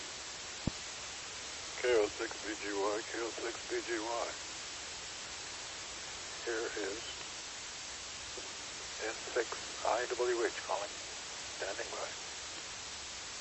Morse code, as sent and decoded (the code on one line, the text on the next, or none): none